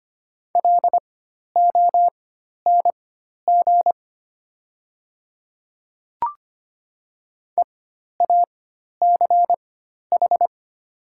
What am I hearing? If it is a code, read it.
Morse code, heard as a keyed tone, about 25 wpm, spaced as for 12 wpm, 700 Hz. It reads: LONG EACH
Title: 4x Character Spacing - Top 100 Words - 25wpm